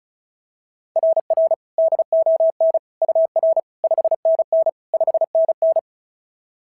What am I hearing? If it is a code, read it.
Morse code, heard as a keyed tone, about 36 wpm, 690 Hz. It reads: RR DON UR 5NN 5NN
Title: SOTA QSO Practice 35wpm